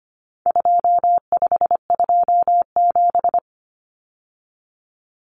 Morse code, heard as a keyed tone, about 25 wpm, 700 Hz. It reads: 2527
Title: Numbers - 4-Digit 25wpm